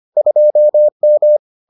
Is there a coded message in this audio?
Yes: Morse code from 0.2 to 1.4 s, heard as a keyed tone, about 25 wpm, 600 Hz. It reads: 2M